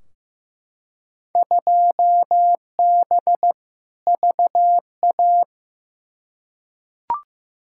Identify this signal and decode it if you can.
Morse code, heard as a keyed tone, about 15 wpm, 700 Hz. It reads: 2B VA
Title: Field Day Exchanges Encore - 15wpm